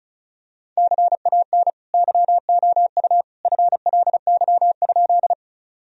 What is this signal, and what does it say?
Morse code, heard as a keyed tone, about 36 wpm, 700 Hz. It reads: CAN YOU FLY?
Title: Sentences from Top 1000 Words - Review Part 2 - 35wpm